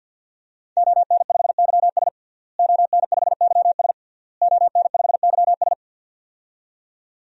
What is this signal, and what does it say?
Morse code, heard as a keyed tone, about 49 wpm, 700 Hz. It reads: ON5YS ON5YS ON5YS
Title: DXCC-100 Call Signs - 5 Characters - Repeated 3x - 50wpm